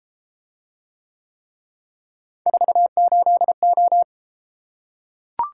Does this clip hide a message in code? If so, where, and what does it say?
Yes: Morse code from 2.5 to 4.0 s, heard as a keyed tone, about 33 wpm, 700 Hz. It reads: 48O